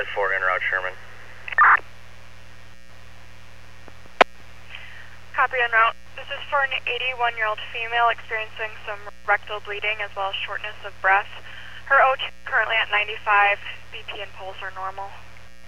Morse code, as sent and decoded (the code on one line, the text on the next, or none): none